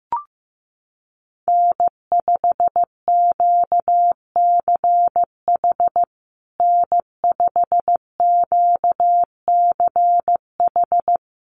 Morse code, heard as a keyed tone, about 15 wpm, 700 Hz. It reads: N5QCH N5QCH
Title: USA Call Signs Repeated 3x 15wpm